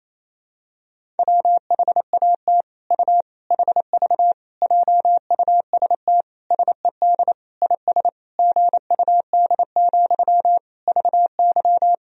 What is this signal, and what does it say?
Morse code, heard as a keyed tone, about 28 wpm, 700 Hz. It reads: WHAT U HV JUST SED IS GUD, VY
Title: Sentences from Top 200 Words and Abbreviations 28wpm